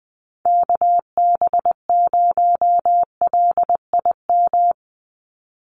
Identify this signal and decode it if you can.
Morse code, heard as a keyed tone, about 20 wpm, 700 Hz. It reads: KB0LIM
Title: USA Call Signs 20wpm